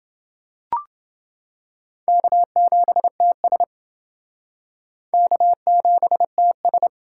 Morse code, heard as a keyed tone, about 31 wpm, 700 Hz. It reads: K7TS K7TS